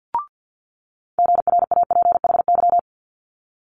Morse code, heard as a keyed tone, about 50 wpm, 700 Hz. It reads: GRAPHY
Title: Multi-letter Phonemes Quick Fire 50wpm